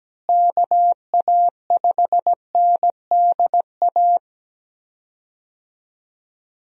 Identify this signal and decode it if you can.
Morse code, heard as a keyed tone, about 17 wpm, 700 Hz. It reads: KA5NDA
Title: USA Call Signs - 6 Characters - Repeated 3x - 17wpm